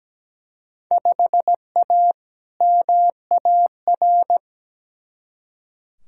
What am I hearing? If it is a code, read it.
Morse code, heard as a keyed tone, about 17 wpm, 700 Hz. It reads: 5A MAR
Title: Field Day Exchanges Encore - 17wpm